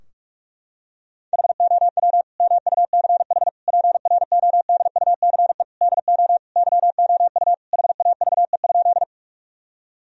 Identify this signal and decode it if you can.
Morse code, heard as a keyed tone, about 46 wpm, 700 Hz. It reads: HOW MUCH PRODUCE DO YOU HAVE?
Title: Sentences from Top 1000 Words - Review Part 3 - 45wpm